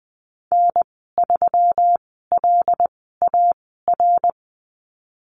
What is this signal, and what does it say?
Morse code, heard as a keyed tone, about 20 wpm, spaced as for 14 wpm, 700 Hz. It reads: N3LAR
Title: 140 Lesson 23g - Letter f - Call Signs 20wpm